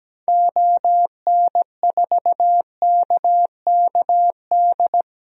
Morse code, heard as a keyed tone, about 17 wpm, 700 Hz. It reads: ON4KKD
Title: DXCC-100 Call Signs - 6 Characters - Repeated 2x - 17wpm